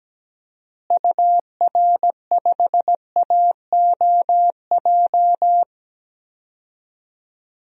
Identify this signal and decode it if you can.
Morse code, heard as a keyed tone, about 17 wpm, 700 Hz. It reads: UR5AOJ
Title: DXCC-100 Call Signs - 6 Characters - Repeated 2x - 17wpm